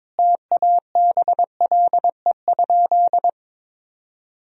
Morse code, heard as a keyed tone, about 22 wpm, 700 Hz. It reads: TABLE?
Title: Sentences from Top 1000 Words - Review Part 4 - 22wpm